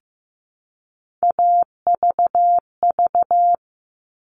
Morse code, heard as a keyed tone, about 15 wpm, 700 Hz. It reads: AVV
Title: All Cut Numbers - 3-Digit 15wpm